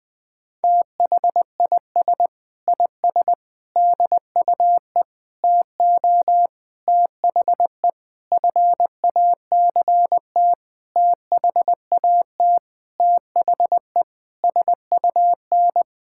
Morse code, heard as a keyed tone, about 20 wpm, 700 Hz. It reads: THIS IS DUE TO THE FACT THAT THE SUN
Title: Sentences with 8-Word Phrases - 20wpm